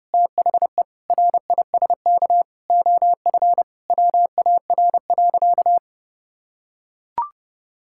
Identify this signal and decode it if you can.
Morse code, heard as a keyed tone, about 32 wpm, 700 Hz. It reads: THE RISK OF WAR.
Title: Sentences from Top 1000 Words - Review Part 6 - 30wpm